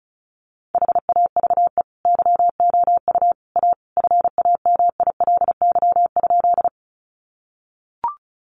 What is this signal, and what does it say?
Morse code, heard as a keyed tone, about 34 wpm, 700 Hz. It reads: HAVE YOU A FAMILY?
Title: Sentences from Top 400 Words 35wpm